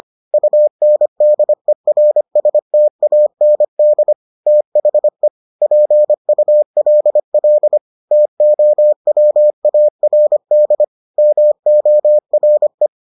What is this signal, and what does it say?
Morse code, heard as a keyed tone, about 25 wpm, 600 Hz. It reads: UNDERSTAND THE PULL TOWARD MORE